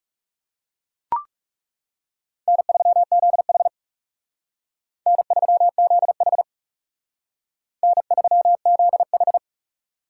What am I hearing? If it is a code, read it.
Morse code, heard as a keyed tone, about 37 wpm, 700 Hz. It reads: N3ZH N3ZH N3ZH
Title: Speed Racing - USA Call Signs 30wpm